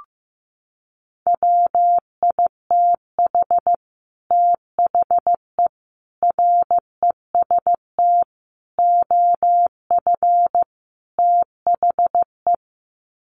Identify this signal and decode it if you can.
Morse code, heard as a keyed tone, about 15 wpm, 700 Hz. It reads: WITH THE REST OF THE
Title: Sets of 5 Words 15wpm